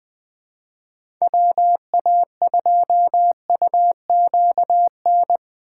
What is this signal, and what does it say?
Morse code, heard as a keyed tone, about 20 wpm, 700 Hz. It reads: WA2UQN
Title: USA Call Signs - 6 Characters - 8x Character Spacing - 20wpm